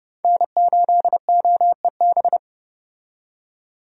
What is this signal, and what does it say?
Morse code, heard as a keyed tone, about 30 wpm, 700 Hz. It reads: N8OEB
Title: USA Call Signs - 5 Characters - 30wpm